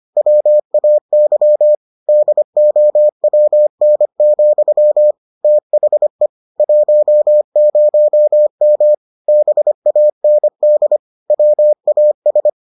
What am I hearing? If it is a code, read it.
Morse code, heard as a keyed tone, about 25 wpm, 600 Hz. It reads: WAY DOWN, THE 10M BAND WAS